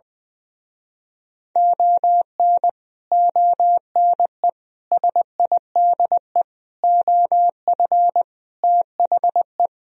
Morse code, heard as a keyed tone, about 20 wpm, 700 Hz. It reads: ON ONE SIDE OF THE
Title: Speed-Racing - Sets of 5 Words 20wpm